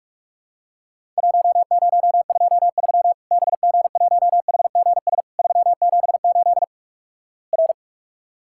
Morse code, heard as a keyed tone, about 46 wpm, 700 Hz. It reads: 1023 BG1HGS 378 R
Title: Worked All Europe DX Contest - QTC Exchange - 45wpm